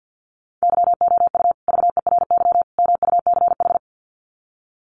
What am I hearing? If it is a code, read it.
Morse code, heard as a keyed tone, about 50 wpm, 700 Hz. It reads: YOU VERY MUCH
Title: Sets of 3 Words 50wpm